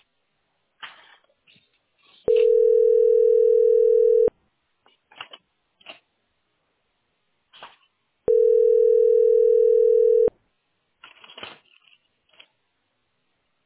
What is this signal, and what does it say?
Ringback tone (North America), 440 + 480 Hz.